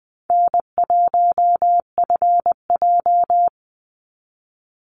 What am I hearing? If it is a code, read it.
Morse code, heard as a keyed tone, about 19 wpm, 700 Hz. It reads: N1FJ
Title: NA SOTA Activator Callsigns 20wpm